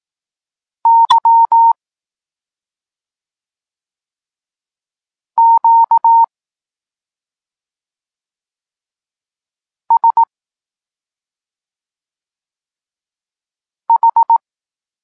Morse code, heard as a keyed tone, about 18 wpm, 920 Hz. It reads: Y Q S H
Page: Tekst 32 – Doskonalenie odbioru poznanych liter